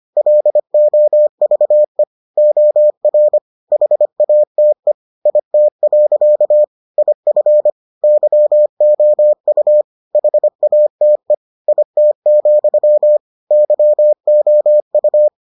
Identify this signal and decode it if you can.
Morse code, heard as a keyed tone, about 25 wpm, 600 Hz. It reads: LOVE OR HATE IT. IF YOU HATE IT, YOU